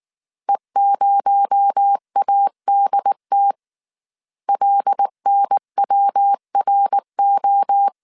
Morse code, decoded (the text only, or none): E0ADT LNWRO